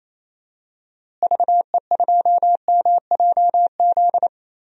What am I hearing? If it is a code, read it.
Morse code, heard as a keyed tone, about 28 wpm, 700 Hz. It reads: VE2MJZ
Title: DXCC-100 Call Signs - 6 Characters - Repeated 2x - 28wpm